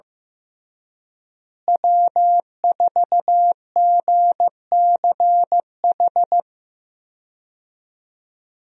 Morse code, heard as a keyed tone, about 16 wpm, 700 Hz. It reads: W4GCH
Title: USA Call Signs - 5 Characters - Repeated 3x - 15wpm